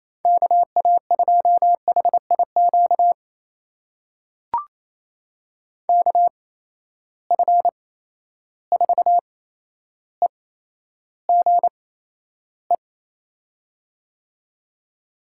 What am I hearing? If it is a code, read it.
Morse code, heard as a keyed tone, about 28 wpm, spaced as for 12 wpm, 700 Hz. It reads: KA2HIQ KF4EGE
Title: USA Call Signs - 8x Character Spacing - 28wpm